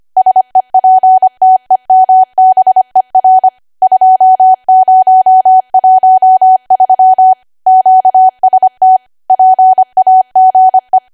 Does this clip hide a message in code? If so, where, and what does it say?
Yes: Morse code from 0.2 to 11.0 s, heard as a keyed tone, about 25 wpm, 750 Hz. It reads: SEPTEMBER 2013 QST PAGE